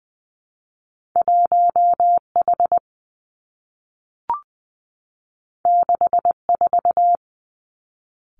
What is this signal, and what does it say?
Morse code, heard as a keyed tone, about 20 wpm, 700 Hz. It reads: 1 H 6 4
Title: Two Challenging Letter-Number Combinations 20wpm